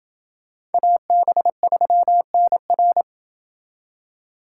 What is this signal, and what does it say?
Morse code, heard as a keyed tone, about 29 wpm, 700 Hz. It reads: AB3NR